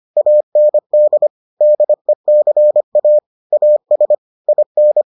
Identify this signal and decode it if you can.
Morse code, heard as a keyed tone, about 25 wpm, 600 Hz. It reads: AND DECA AS IN